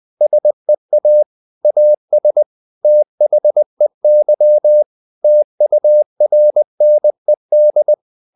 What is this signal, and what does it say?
Morse code, heard as a keyed tone, about 20 wpm, 600 Hz. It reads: SEA AS THEY TURNED
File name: the old man and the sea part 2 20wpm